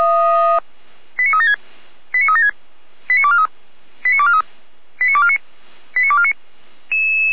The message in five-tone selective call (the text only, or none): 89376 89376 89243 89243 89249 89249